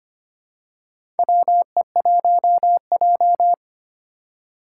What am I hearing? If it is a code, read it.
Morse code, heard as a keyed tone, about 25 wpm, 700 Hz. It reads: WE1J
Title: USA Call Signs - 4 Characters - 2x Character Spacing - 25wpm